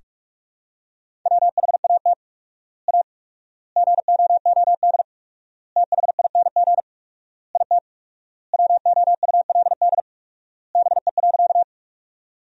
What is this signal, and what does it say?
Morse code, heard as a keyed tone, about 47 wpm, 700 Hz. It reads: WHAT A GOOD THING IT WOULD BE.
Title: 4x Word-Spacing - Sentences from Top 200 Words - 45wpm